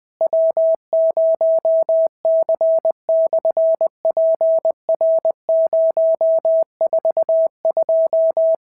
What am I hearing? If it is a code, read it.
Morse code, heard as a keyed tone, about 20 wpm, 650 Hz. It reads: W0C/PR042
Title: SOTA QSO Practice 20wpm